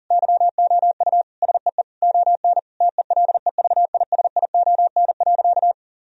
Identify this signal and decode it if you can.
Morse code, heard as a keyed tone, about 40 wpm, 700 Hz. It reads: YOU SEE ON TELEVISION.
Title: Sentences from Top 1000 Words - Review Part 4 - 40wpm